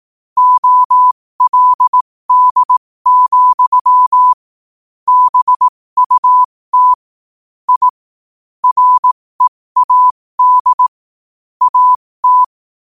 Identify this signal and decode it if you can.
Morse code, heard as a keyed tone, about 18 wpm, spaced as for 15 wpm, 1000 Hz. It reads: OLD, BUT I READ AT
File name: qotdfiles / 15wpm